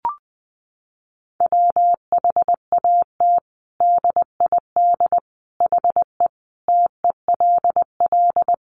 Morse code, heard as a keyed tone, about 20 wpm, 700 Hz. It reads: WHAT DID HE TELL
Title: Sentences from Top 200 Words 20wpm